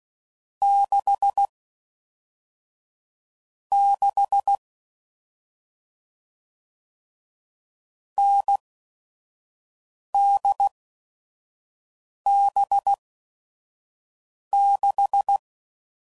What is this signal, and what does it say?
Morse code, heard as a keyed tone, about 16 wpm, spaced as for 5 wpm, 800 Hz. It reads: 6 6 NDB6
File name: Combinations / Difficult Combos / awj1 ndb6 gz7